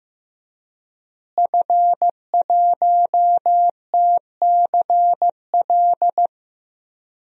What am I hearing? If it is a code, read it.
Morse code, heard as a keyed tone, about 16 wpm, 700 Hz. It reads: F1TCL
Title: DXCC-100 Call Signs - 5 Characters - Repeated 3x - 15wpm